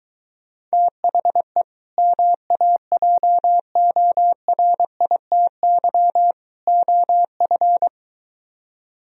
Morse code, heard as a keyed tone, about 23 wpm, 700 Hz. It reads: THE MAJORITY OF